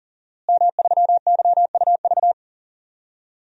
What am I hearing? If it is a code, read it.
Morse code, heard as a keyed tone, about 39 wpm, 690 Hz. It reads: M3YUV
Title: DXCC-100 Call Signs - 5 Characters - Repeated 2x - 40wpm